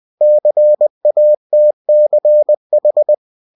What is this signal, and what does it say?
Morse code, heard as a keyed tone, about 20 wpm, 600 Hz. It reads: CATCH
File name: winnie-20wpm